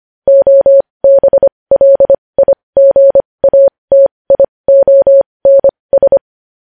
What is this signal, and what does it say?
Morse code, heard as a keyed tone, about 25 wpm, spaced as for 20 wpm, 550 Hz. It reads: OBLIGATIONS
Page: News headlines in Morse code at 20 wpm